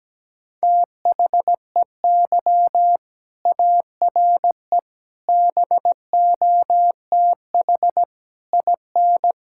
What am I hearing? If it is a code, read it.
Morse code, heard as a keyed tone, about 17 wpm, 700 Hz. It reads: THEY ARE BOTH IN
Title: Sentences from Top 1000 Words - Review Part 5 - 17wpm